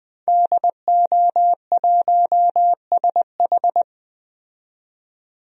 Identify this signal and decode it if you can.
Morse code, heard as a keyed tone, about 20 wpm, 700 Hz. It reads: DO1SH
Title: DXCC-100 Call Signs - 5 Characters - 20wpm